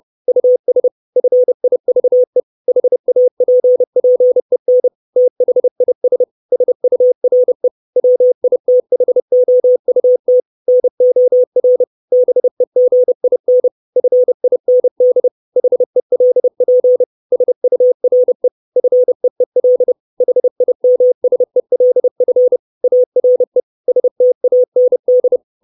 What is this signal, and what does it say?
Morse code, heard as a keyed tone, about 30 wpm, 500 Hz. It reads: US FIVE HAPPEN THIS SURE WITHOUT NOR BEGIN FIND HELP SURE FEEL HIMSELF ARE STAND
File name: 632 Words 30WPM